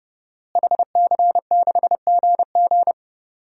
Morse code, heard as a keyed tone, about 30 wpm, 700 Hz. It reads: HC6GG
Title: DXCC-100 Call Signs - 5 Characters - Repeated 2x - 30wpm